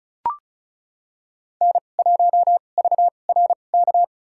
Morse code, heard as a keyed tone, about 36 wpm, spaced as for 25 wpm, 700 Hz. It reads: N1VRK
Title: USA Call Signs - 5 Characters - 2x Character Spacing - 35wpm